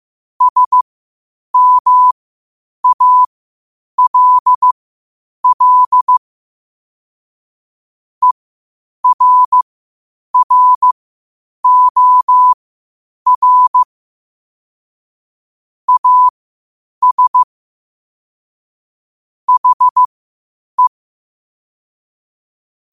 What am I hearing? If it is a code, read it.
Morse code, heard as a keyed tone, about 15 wpm, spaced as for 8 wpm, 1000 Hz. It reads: SMALL ERROR AS HE
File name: qotdfiles / 8wpm